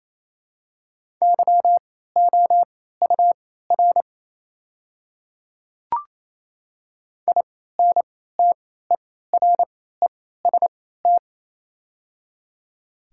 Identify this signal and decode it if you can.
Morse code, heard as a keyed tone, about 28 wpm, spaced as for 16 wpm, 700 Hz. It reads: YOUR INTEREST
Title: 3x Character Spacing - Top 500 Words - 28wpm